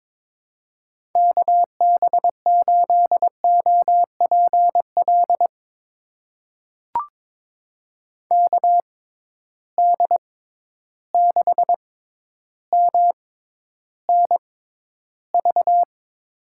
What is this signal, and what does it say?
Morse code, heard as a keyed tone, about 21 wpm, spaced as for 11 wpm, 700 Hz. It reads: KB8OPL KD6MNV
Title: USA Call Signs - 6 Characters - 6x Character Spacing - 22wpm